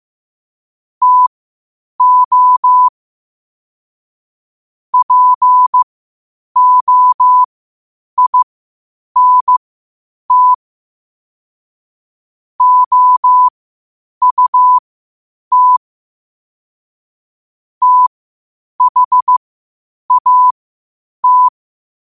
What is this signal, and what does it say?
Morse code, heard as a keyed tone, about 15 wpm, spaced as for 8 wpm, 1000 Hz. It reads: TO POINT OUT THAT
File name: qotdfiles / 8wpm